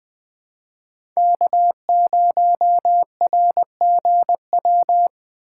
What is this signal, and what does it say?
Morse code, heard as a keyed tone, about 20 wpm, 700 Hz. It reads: K0RGW
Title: USA Call Signs - 5 Characters - 4x Character Spacing - 20wpm